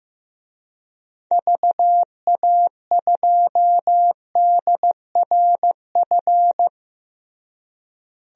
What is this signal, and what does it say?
Morse code, heard as a keyed tone, about 16 wpm, 700 Hz. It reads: VA2DRF
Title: DXCC-100 Call Signs - 6 Characters - Repeated 3x - 15wpm